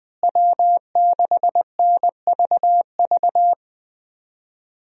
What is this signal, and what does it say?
Morse code, heard as a keyed tone, about 20 wpm, 700 Hz. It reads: W6NVV
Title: USA Call Signs - 5 Characters - Repeated 3x - 20wpm